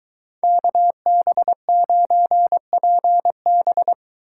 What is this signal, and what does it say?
Morse code, heard as a keyed tone, about 22 wpm, 700 Hz. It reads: KB9PB